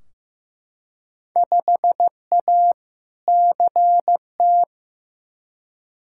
Morse code, heard as a keyed tone, about 15 wpm, 700 Hz. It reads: 5A CT
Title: Field Day Exchanges - 15wpm